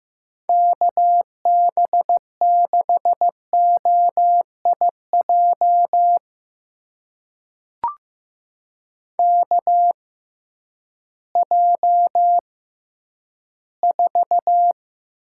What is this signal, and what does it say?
Morse code, heard as a keyed tone, about 15 wpm, spaced as for 10 wpm, 700 Hz. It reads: KB6OIJ KJ4
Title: USA Call Signs - 6 Characters - 6x Character Spacing - 15wpm